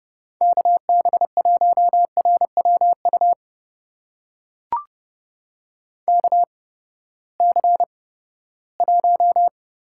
Morse code, heard as a keyed tone, about 31 wpm, spaced as for 17 wpm, 700 Hz. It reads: KB1RWU KC1
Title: USA Call Signs - 6 Characters - 8x Character Spacing - 30wpm